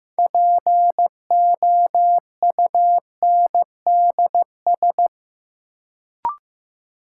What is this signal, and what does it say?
Morse code, heard as a keyed tone, about 15 wpm, 700 Hz. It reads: POUNDS
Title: Sentences from Top 1000 Words - Review Part 2 - New Word Drill - 15wpm